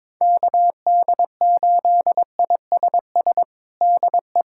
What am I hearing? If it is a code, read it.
Morse code, heard as a keyed tone, about 22 wpm, 700 Hz. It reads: KD8ISS DE
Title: USA Callsign DE Callsign - 6 Characters - 22wpm